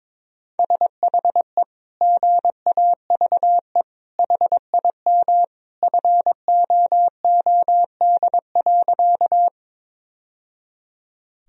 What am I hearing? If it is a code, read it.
Morse code, heard as a keyed tone, about 22 wpm, 700 Hz. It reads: SHE GAVE HIM FOOD.
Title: Sentences from Top 1000 Words - Review Part 1 - 22wpm